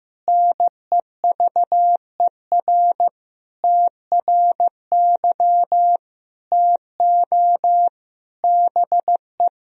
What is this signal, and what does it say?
Morse code, heard as a keyed tone, about 15 wpm, 700 Hz. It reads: NEVER TRY TO BE
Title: Sentences from Top 1000 Words - Review Part 3 - 15wpm